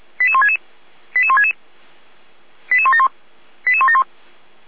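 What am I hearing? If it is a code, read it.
Five-tone selective call (ZVEI), heard as quick runs of five tones, tones of about 70 ms. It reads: 88166 88166 88171 88171
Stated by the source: ZVEI five-tone call received via SDR.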